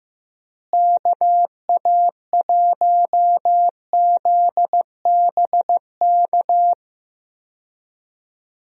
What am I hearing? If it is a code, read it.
Morse code, heard as a keyed tone, about 15 wpm, 700 Hz. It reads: KA1ZBK
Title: USA Call Signs - 6 Characters - Repeated 3x - 15wpm